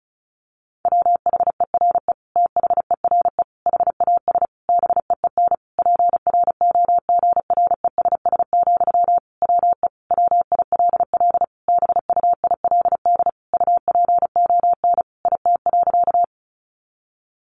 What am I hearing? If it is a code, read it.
Morse code, heard as a keyed tone, about 34 wpm, 700 Hz. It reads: WHERE THERE HAS BEEN PROGRESS, WE WILL BUILD UPON IT.
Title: Sentences from Top 1400 Words 35wpm